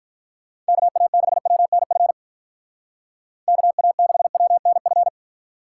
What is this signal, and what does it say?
Morse code, heard as a keyed tone, about 43 wpm, 700 Hz. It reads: KA6WNF KA6WNF
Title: Speed Racing - USA 2x3 Call Signs 35wpm